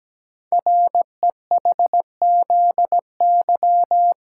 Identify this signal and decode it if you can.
Morse code, heard as a keyed tone, about 17 wpm, 700 Hz. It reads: REHZY
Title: Code Groups - 5 Letters - 17wpm